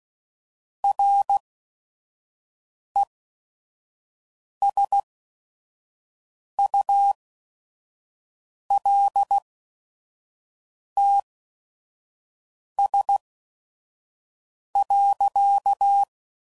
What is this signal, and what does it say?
Morse code, heard as a keyed tone, about 16 wpm, spaced as for 5 wpm, 800 Hz. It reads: RESULTS.
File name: CW Code Groups @5WPM / Technical Subjects / Gates Intro